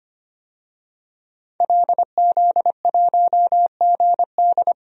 Morse code, heard as a keyed tone, about 25 wpm, 700 Hz. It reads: LZ1GD
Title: DXCC-100 Call Signs - 5 Characters - 25wpm